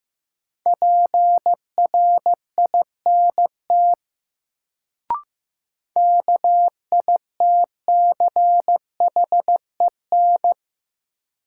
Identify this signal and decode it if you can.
Morse code, heard as a keyed tone, about 15 wpm, 700 Hz. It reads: PRINT KITCHEN
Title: Sentences from Top 1400 Words - New Word Drill Alt - 15wpm